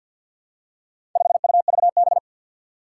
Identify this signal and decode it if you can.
Morse code, heard as a keyed tone, about 52 wpm, 700 Hz. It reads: 5UVB
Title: Four Challenging Letter-Number Combinations 50wpm